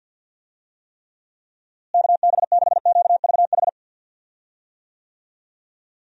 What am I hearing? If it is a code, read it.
Morse code, heard as a keyed tone, about 50 wpm, 700 Hz. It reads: KB6QVH
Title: USA Call Signs - 6 Characters - Repeated 3x - 50wpm